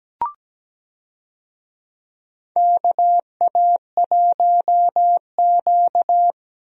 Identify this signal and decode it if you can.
Morse code, heard as a keyed tone, about 17 wpm, 700 Hz. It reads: KA1Q
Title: USA Call Signs - 4 Characters - Repeated 2x - 17wpm